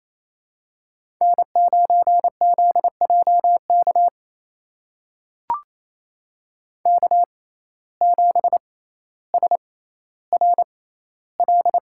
Morse code, heard as a keyed tone, about 28 wpm, spaced as for 14 wpm, 700 Hz. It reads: N9ZJK K7SRL
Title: USA Call Signs - 5 Characters - 6x Character Spacing - 28wpm